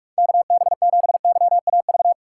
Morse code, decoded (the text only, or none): KB7YAV